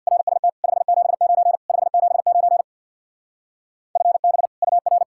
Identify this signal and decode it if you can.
Morse code, heard as a keyed tone, about 64 wpm, 700 Hz. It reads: RST 579 579 FB UR